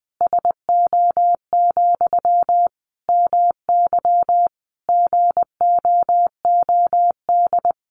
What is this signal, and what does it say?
Morse code, heard as a keyed tone, about 20 wpm, 700 Hz. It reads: SO, MY GOOD
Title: Sentences from Top 1200 Words 20wpm